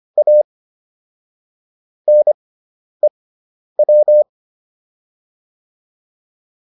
Morse code, heard as a keyed tone, about 25 wpm, spaced as for 10 wpm, 600 Hz. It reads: A NEW